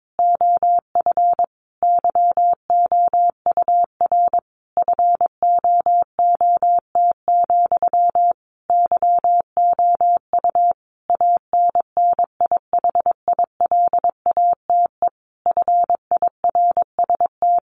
Morse code, heard as a keyed tone, about 22 wpm, 700 Hz. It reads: OF YOUR FOOT, YOU ANNIHILATE FIRST
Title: A Sound Of Thunder 22wpm